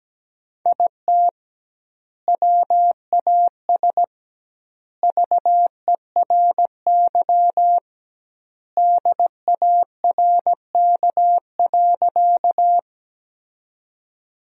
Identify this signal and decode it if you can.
Morse code, heard as a keyed tone, about 17 wpm, 700 Hz. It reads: IT WAS VERY DARK.
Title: 2x Word-Spacing - Sentences from Top 500 Words - 17wpm